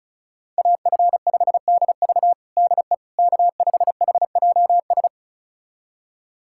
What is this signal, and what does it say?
Morse code, heard as a keyed tone, about 36 wpm, 700 Hz. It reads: AF5DV DE K5HJS
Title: USA Callsign DE Callsign - 5 Characters - 35wpm